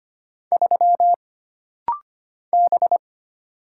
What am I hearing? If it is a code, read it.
Morse code, heard as a keyed tone, about 25 wpm, 700 Hz. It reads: 3 B
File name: Single Character (Letter:Number:Punctuation:Prosign) - Rapid-Fire - 25wpm